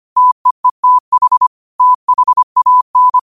Morse code, heard as a keyed tone, about 25 wpm, 1000 Hz. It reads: TEETH THAN